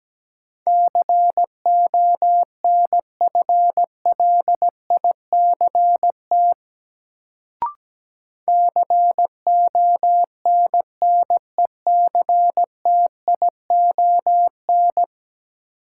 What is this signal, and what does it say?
Morse code, heard as a keyed tone, about 17 wpm, 700 Hz. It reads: CONFLICT CONNECTION
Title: Sentences from Top 1300 Words - New Word Drill Alt - 17wpm